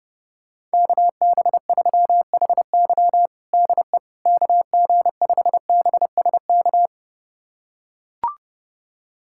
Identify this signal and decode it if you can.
Morse code, heard as a keyed tone, about 32 wpm, 700 Hz. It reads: KB3HY DE KG5BSK
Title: USA Callsign DE Callsign 30wpm